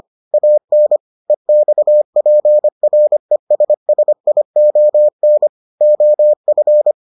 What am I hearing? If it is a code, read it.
Morse code, heard as a keyed tone, about 25 wpm, 600 Hz. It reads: AN EXPRESSION OF